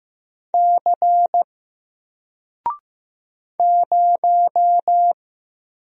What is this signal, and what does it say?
Morse code, heard as a keyed tone, about 15 wpm, 700 Hz. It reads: C 0
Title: Single Letter-Number 15wpm